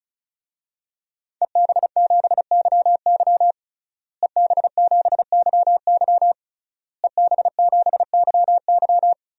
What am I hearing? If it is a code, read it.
Morse code, heard as a keyed tone, about 36 wpm, 700 Hz. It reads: EB7YY EB7YY EB7YY
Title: DXCC-100 Call Signs - 5 Characters - Repeated 3x - 35wpm